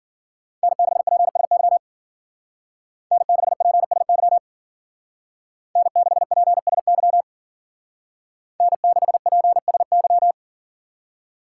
Morse code, heard as a keyed tone, about 52 wpm, 700 Hz. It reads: N6PSY N6PSY N6PSY N6PSY